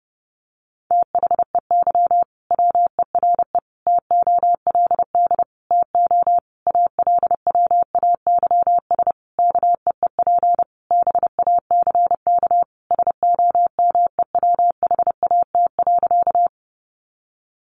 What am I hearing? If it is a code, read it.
Morse code, heard as a keyed tone, about 30 wpm, 700 Hz. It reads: THEY WERE TOLD TO ALWAYS KEEP BACK SOMEWHAT.
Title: Sentences from Top 1300 Words 30wpm